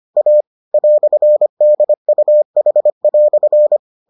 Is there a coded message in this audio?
Yes: Morse code from 0.2 to 3.8 s, heard as a keyed tone, about 25 wpm, 600 Hz. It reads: A "DUH"